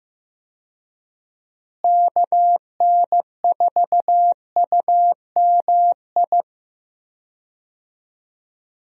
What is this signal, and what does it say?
Morse code, heard as a keyed tone, about 15 wpm, 700 Hz. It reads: KN4UMI
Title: USA Call Signs - 6 Characters - Repeated 3x - 15wpm